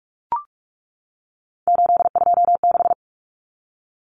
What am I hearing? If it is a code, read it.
Morse code, heard as a keyed tone, about 44 wpm, 700 Hz. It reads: 826
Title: Common Cut Numbers - 3-Digit 45wpm